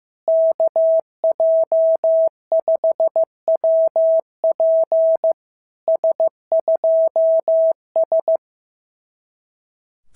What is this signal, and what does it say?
Morse code, heard as a keyed tone, about 15 wpm, 650 Hz. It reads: KJ5WP S2S
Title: SOTA QSO Practice 15wpm